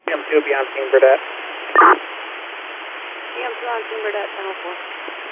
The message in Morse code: none

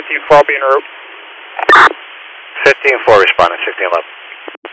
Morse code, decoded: none

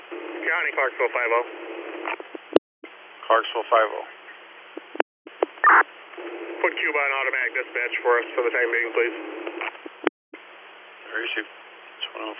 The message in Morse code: none

